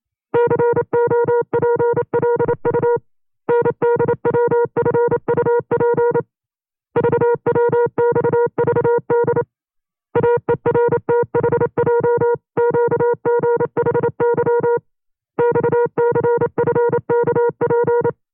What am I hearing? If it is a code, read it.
Morse code, heard as a keyed tone, about 28 wpm, 480 Hz. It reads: COPLU NDWFUP VWXVD AERTHJQGHY XCFKP
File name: 25WPM